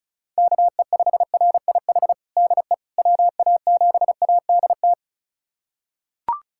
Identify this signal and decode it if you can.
Morse code, heard as a keyed tone, about 36 wpm, 700 Hz. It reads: KE5RIH DE WA7ADT
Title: USA Callsign DE Callsign - 6 Characters - 35wpm